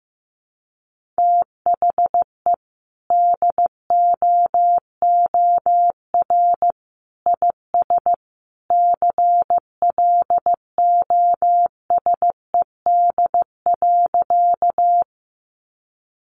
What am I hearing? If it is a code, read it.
Morse code, heard as a keyed tone, about 14 wpm, 700 Hz. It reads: THE DOOR IS CLOSED.
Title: Sentences from Top 1400 Words 15wpm